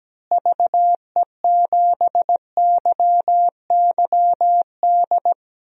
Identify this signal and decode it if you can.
Morse code, heard as a keyed tone, about 17 wpm, 700 Hz. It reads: VE7YYD
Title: DXCC-100 Call Signs - 6 Characters - Repeated 2x - 17wpm